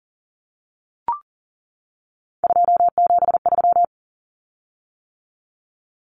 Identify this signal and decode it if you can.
Morse code, heard as a keyed tone, about 40 wpm, 700 Hz. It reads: 273
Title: Common Cut Numbers - 3-Digit 40wpm